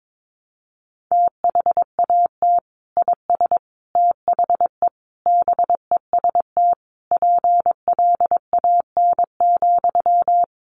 Morse code, heard as a keyed tone, about 22 wpm, 700 Hz. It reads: THAT IS THE BEST PLAN,
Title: Sentences from Top 1000 Words 22wpm